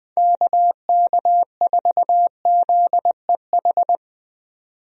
Morse code, heard as a keyed tone, about 20 wpm, 700 Hz. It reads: KK4ZEH
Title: USA Call Signs - 6 Characters - 20wpm